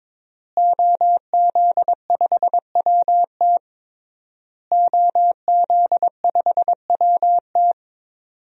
Morse code, heard as a keyed tone, about 22 wpm, 700 Hz. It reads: OZ5WT OZ5WT
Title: DXCC-100 Call Signs - 5 Characters - Repeated 3x - 22wpm